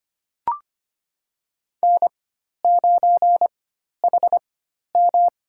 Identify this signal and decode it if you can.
Morse code, heard as a keyed tone, about 25 wpm, spaced as for 12 wpm, 700 Hz. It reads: N9HM
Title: USA Call Signs - 4 Characters - 4x Character Spacing - 25wpm